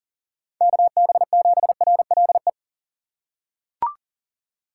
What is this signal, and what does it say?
Morse code, heard as a keyed tone, about 40 wpm, 700 Hz. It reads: KB7RLE
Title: USA Call Signs - 6 Characters - 6x Character Spacing - 40wpm